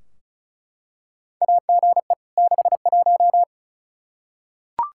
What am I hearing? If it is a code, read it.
Morse code, heard as a keyed tone, about 34 wpm, 700 Hz. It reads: AGE 61
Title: Practice QSO - Exchange 3 - More Info - 35wpm